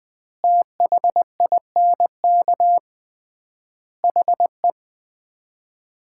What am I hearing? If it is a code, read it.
Morse code, heard as a keyed tone, about 21 wpm, 700 Hz. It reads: THINK HE
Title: 3x Word-Spacing - Sets of 4 Words - 20wpm